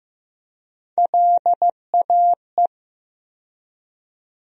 Morse code, heard as a keyed tone, about 15 wpm, 700 Hz. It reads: LAE
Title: USA Call Sign Postfixes - 3 Letters 15wpm